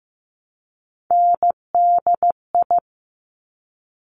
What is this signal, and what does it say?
Morse code, heard as a keyed tone, about 15 wpm, 700 Hz. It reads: NDI
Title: Top Three Letter Combinations in Words 15wpm